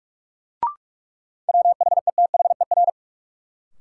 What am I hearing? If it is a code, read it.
Morse code, heard as a keyed tone, about 45 wpm, 700 Hz. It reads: WHETHER
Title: Sets of 3 Words - New Word Drill - 45wpm